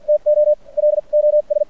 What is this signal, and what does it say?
Morse code, heard as a keyed tone, about 50 wpm, spaced as for 41 wpm, 600 Hz. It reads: TO LOS
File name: archived / CRSnet